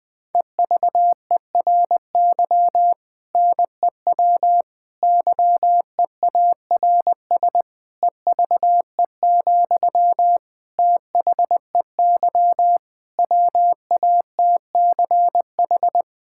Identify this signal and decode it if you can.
Morse code, heard as a keyed tone, about 20 wpm, 700 Hz. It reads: EVERY NEW YEARS EVE, THEY WATCH
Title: Difficult Sentences with Hints - 20wpm